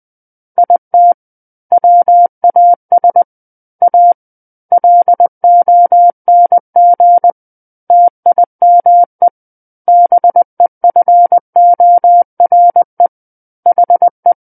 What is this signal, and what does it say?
Morse code, heard as a keyed tone, about 20 wpm, 700 Hz. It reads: IT WAS A LONG TIME BEFORE HE
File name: 20 WPM